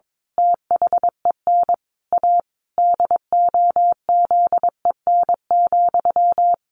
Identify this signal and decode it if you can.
Morse code, heard as a keyed tone, about 22 wpm, 700 Hz. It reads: THEN A DOZEN,
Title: A Sound Of Thunder 22wpm